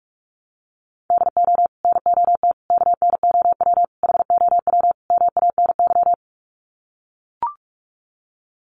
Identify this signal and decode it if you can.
Morse code, heard as a keyed tone, about 44 wpm, 700 Hz. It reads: DO NOT KNOW HOW MANY
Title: Sets of 5 Words Encore 45wpm